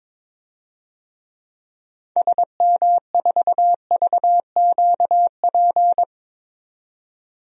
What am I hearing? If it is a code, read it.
Morse code, heard as a keyed tone, about 22 wpm, 700 Hz. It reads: SM4VQP
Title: DXCC-100 Call Signs - 6 Characters - 22wpm